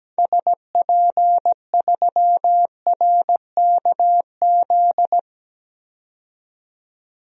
Morse code, heard as a keyed tone, about 17 wpm, 700 Hz. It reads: SP3RKZ
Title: DXCC-100 Call Signs - 6 Characters - 17wpm